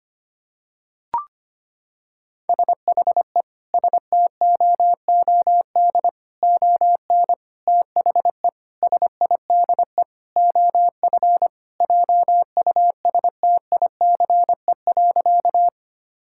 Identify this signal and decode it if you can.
Morse code, heard as a keyed tone, about 25 wpm, 700 Hz. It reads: SHE STOOD ON THE SIDE OF JUSTICE.
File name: practice / 25wpm / Sentences with 4-Word Phrases - 25wpm